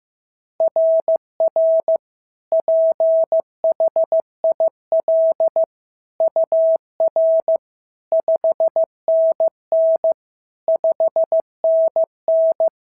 Morse code, heard as a keyed tone, about 14 wpm, 650 Hz. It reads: RR PHIL UR 5NN 5NN
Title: SOTA QSO Practice 15wpm